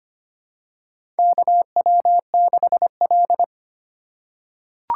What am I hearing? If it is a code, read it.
Morse code, heard as a keyed tone, about 25 wpm, 700 Hz. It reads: KW6L
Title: USA Call Signs - 4 Characters - 8x Character Spacing - 25wpm